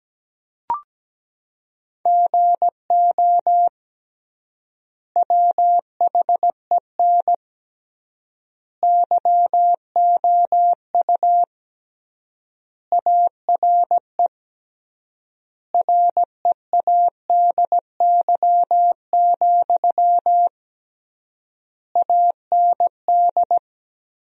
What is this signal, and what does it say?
Morse code, heard as a keyed tone, about 17 wpm, 700 Hz. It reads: GO WHEN YOU ARE READY, AND
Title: 3x Word-Spacing - Sentences from Top 500 Words - 17wpm